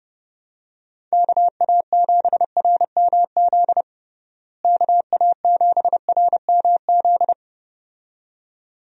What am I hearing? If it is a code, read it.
Morse code, heard as a keyed tone, about 32 wpm, 700 Hz. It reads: KA7RMZ KA7RMZ
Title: USA Call Signs - 6 Characters - Repeated 2x - 30wpm